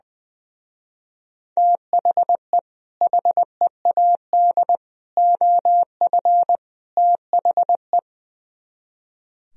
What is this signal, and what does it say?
Morse code, heard as a keyed tone, about 21 wpm, 700 Hz. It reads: THE HEAD OF THE
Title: Speed-Racing - Sets of 4 Words 20wpm